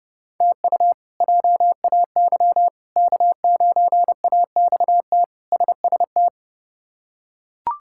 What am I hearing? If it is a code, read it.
Morse code, heard as a keyed tone, about 30 wpm, 700 Hz. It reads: TU JAY K9AXT SST
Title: K1USN SST QSO Practice - 30wpm